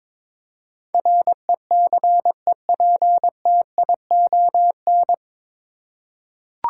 Morse code, heard as a keyed tone, about 22 wpm, 700 Hz. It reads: RECEPTION
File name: 9 Letter Words - First 3 Letters Given - 22wpm